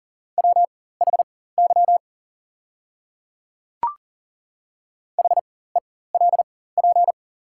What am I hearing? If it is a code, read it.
Morse code, heard as a keyed tone, about 42 wpm, spaced as for 19 wpm, 690 Hz. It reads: WHY HELP
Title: 4x Character Spacing - Top 200 Words - 40wpm